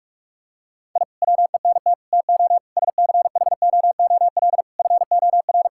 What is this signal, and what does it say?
Morse code, heard as a keyed tone, about 46 wpm, 700 Hz. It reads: I WENT TO SCHOOL FOR
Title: Sentences from Top 1000 Words - Review Part 6 - 45wpm